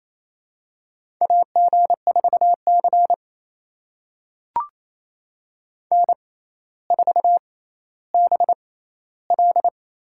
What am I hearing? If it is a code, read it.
Morse code, heard as a keyed tone, about 28 wpm, spaced as for 14 wpm, 700 Hz. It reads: AG4C N4BL
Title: USA Call Signs - 4 Characters - 6x Character Spacing - 28wpm